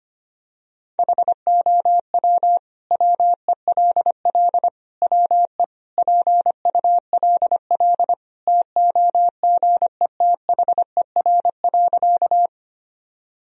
Morse code, heard as a keyed tone, about 25 wpm, 700 Hz. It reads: HOW WELL WE PULL TOGETHER.
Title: Sentences from Top 1000 Words - Review Part 2 - 25wpm